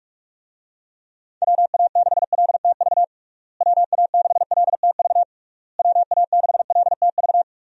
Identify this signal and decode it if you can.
Morse code, heard as a keyed tone, about 46 wpm, 700 Hz. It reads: WA6LTV WA6LTV WA6LTV
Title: USA Call Signs - 6 Characters - Repeated 3x - 45wpm